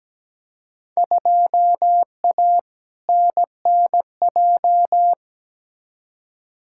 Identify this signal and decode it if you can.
Morse code, heard as a keyed tone, about 17 wpm, 700 Hz. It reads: 2A NNJ
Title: Field Day Exchanges - 17wpm